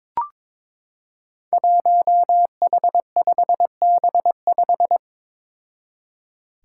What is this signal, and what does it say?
Morse code, heard as a keyed tone, about 22 wpm, 700 Hz. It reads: 1H5B5
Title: Five Challenging Letter-Number Combinations 22wpm